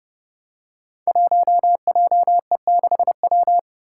Morse code, heard as a keyed tone, about 30 wpm, 700 Hz. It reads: 1JE6W
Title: Five Challenging Letter-Number Combinations 30wpm